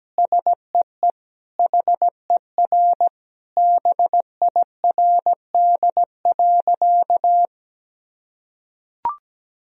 Morse code, heard as a keyed tone, about 17 wpm, 700 Hz. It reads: SEE HER BIRD.
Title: Sentences from Top 1000 Words - Review Part 2 - 17wpm